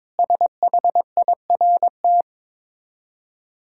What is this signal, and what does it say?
Morse code, heard as a keyed tone, about 23 wpm, 690 Hz. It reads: SHIRT
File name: Difficult Words with Hints - 22wpm